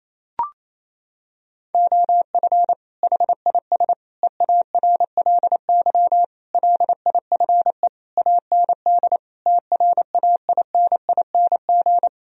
Morse code, heard as a keyed tone, about 28 wpm, 700 Hz. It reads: OF HIS EARLY LIFE AND TRAINING
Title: Sentences from Top 1000 Words - Review Part 4 - 28wpm